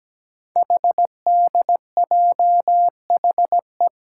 Morse code, heard as a keyed tone, about 17 wpm, 700 Hz. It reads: HDJHE
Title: Code Groups - 5 Letters - 17wpm